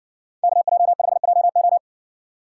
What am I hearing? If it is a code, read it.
Morse code, heard as a keyed tone, about 60 wpm, 700 Hz. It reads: KJ5JO